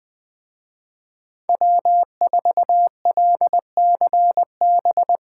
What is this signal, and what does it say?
Morse code, heard as a keyed tone, about 23 wpm, 700 Hz. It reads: W4LCB
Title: Speed Racing - USA 1x3 Call Signs 20wpm